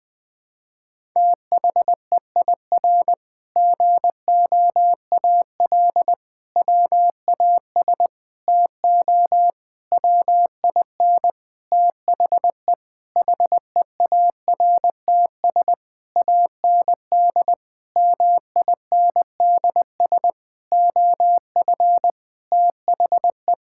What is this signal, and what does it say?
Morse code, heard as a keyed tone, about 20 wpm, 700 Hz. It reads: THEIR GOAL WAS TO WIN THE HEARTS AND MINDS OF THE
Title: Sentences with 8-Word Phrases - Spoken Beginning - 20wpm